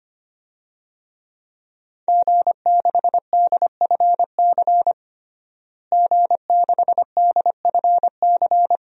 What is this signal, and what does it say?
Morse code, heard as a keyed tone, about 25 wpm, 700 Hz. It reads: G6DFC G6DFC
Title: DXCC-100 Call Signs - 5 Characters - Repeated 3x - 25wpm